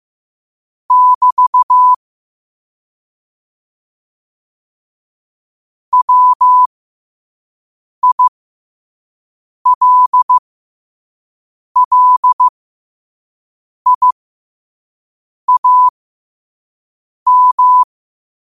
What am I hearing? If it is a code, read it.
Morse code, heard as a keyed tone, about 15 wpm, spaced as for 5 wpm, 1000 Hz. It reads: <BT> WILLIAM
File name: qotdfiles / 5wpm